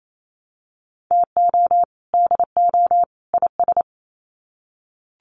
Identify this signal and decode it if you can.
Morse code, heard as a keyed tone, about 28 wpm, 700 Hz. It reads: TO DO IS
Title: Sets of 3 Words 28wpm